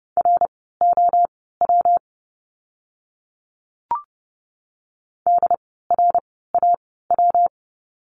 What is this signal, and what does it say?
Morse code, heard as a keyed tone, about 30 wpm, spaced as for 17 wpm, 700 Hz. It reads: ROW DRAW
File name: course / 30:10wpm / 115 Lesson 19f - Letter w - Words 30wpm